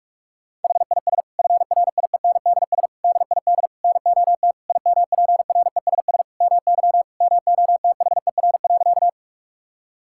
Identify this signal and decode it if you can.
Morse code, heard as a keyed tone, about 41 wpm, 700 Hz. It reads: HIS FRIENDS DID NOT IMPRESS MY MOTHER.
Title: Sentences from Top 1600 Words 45wpm